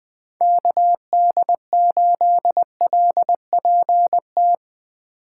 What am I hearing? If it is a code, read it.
Morse code, heard as a keyed tone, about 20 wpm, 700 Hz. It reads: KD8LPT
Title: USA Call Signs - 8x Character Spacing - 20wpm